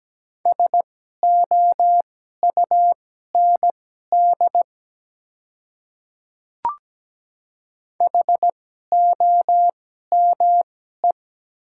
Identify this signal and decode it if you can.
Morse code, heard as a keyed tone, about 17 wpm, spaced as for 12 wpm, 690 Hz. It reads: SOUND HOME
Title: 2x Character Spacing - Top 300 Words - 17wpm